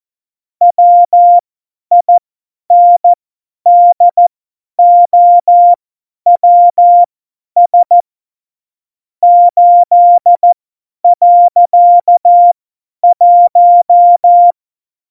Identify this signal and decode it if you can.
Morse code, heard as a keyed tone, about 14 wpm, spaced as for 10 wpm, 700 Hz. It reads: WINDOWS 8.1
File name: Morsekurs / Witze / Witz 29. Januar 2019, 50 F70